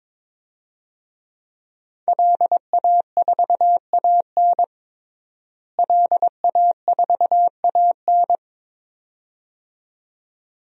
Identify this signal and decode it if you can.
Morse code, heard as a keyed tone, about 22 wpm, 700 Hz. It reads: LA4AN LA4AN
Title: DXCC-100 Call Signs - 5 Characters - Repeated 2x - 22wpm